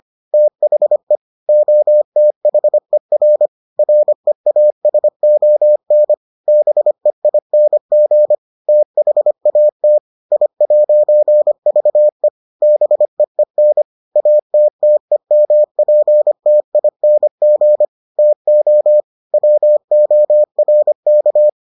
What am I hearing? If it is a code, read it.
Morse code, heard as a keyed tone, about 25 wpm, 600 Hz. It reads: THE OTHER REASON BEING THAT I'VE BEEN ATTEMPTING TO WORK